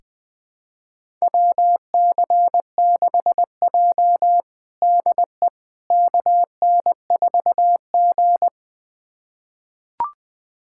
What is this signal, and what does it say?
Morse code, heard as a keyed tone, about 20 wpm, 700 Hz. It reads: WC6J DE KN4G
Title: USA Callsign DE Callsign - 4 Characters - 20wpm